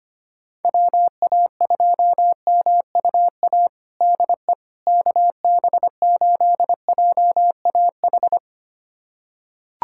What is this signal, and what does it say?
Morse code, heard as a keyed tone, about 25 wpm, 700 Hz. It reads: WA2MUA DE KB8JAH
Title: USA Callsign DE Callsign 25wpm